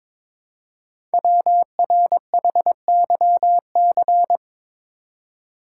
Morse code, heard as a keyed tone, about 22 wpm, 700 Hz. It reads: WRHYC
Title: Code Groups - 5 Letters - 22wpm